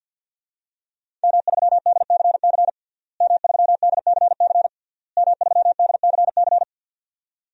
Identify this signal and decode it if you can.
Morse code, heard as a keyed tone, about 47 wpm, 700 Hz. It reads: M3DCC M3DCC M3DCC
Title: DXCC-100 Call Signs - 5 Characters - Repeated 3x - 50wpm